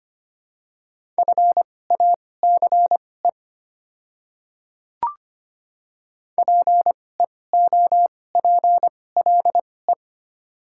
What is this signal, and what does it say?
Morse code, heard as a keyed tone, about 25 wpm, spaced as for 18 wpm, 690 Hz. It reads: FACE PEOPLE
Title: 2x Character Spacing - Top 300 Words - 25wpm